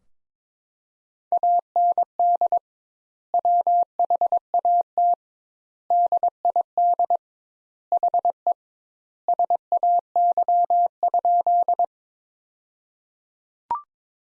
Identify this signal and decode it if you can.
Morse code, heard as a keyed tone, about 22 wpm, 700 Hz. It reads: AND WHAT DID HE SAY?
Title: Sentences Top 200 Words - 22wpm